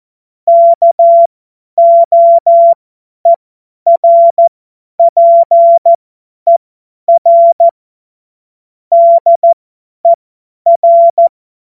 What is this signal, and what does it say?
Morse code, heard as a keyed tone, about 14 wpm, spaced as for 10 wpm, 680 Hz. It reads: KOERPER DER
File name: Morsekurs / Witze / Witz 25. Oktober 2022 50 F70